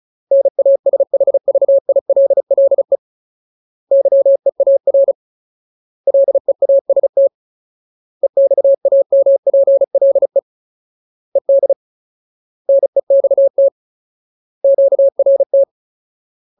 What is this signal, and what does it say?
Morse code, heard as a keyed tone, about 34 wpm, 550 Hz. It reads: NASHVILLE YEAR LEAST EXAMPLE ED NEXT QRT
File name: Ultimate Word List / 35WPM Ultimate